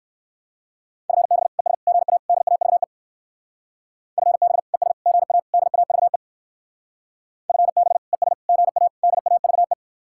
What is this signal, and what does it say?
Morse code, heard as a keyed tone, about 59 wpm, 700 Hz. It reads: FB ES GA DAVE FB ES GA DAVE FB ES GA DAVE